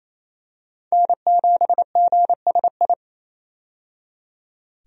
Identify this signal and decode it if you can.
Morse code, heard as a keyed tone, about 27 wpm, 700 Hz. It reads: N7GSI
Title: USA Call Signs - 5 Characters - 28wpm